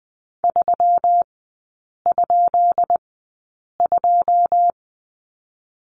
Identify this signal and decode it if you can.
Morse code, heard as a keyed tone, about 20 wpm, spaced as for 8 wpm, 700 Hz. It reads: 3?2
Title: 133 Lesson 22e - Question Mark - Three Characters 20wpm